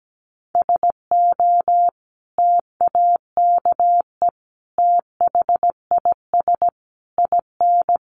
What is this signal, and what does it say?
Morse code, heard as a keyed tone, about 17 wpm, 700 Hz. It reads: SO TAKE THIS IN
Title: Sentences from Top 1500 Words 17wpm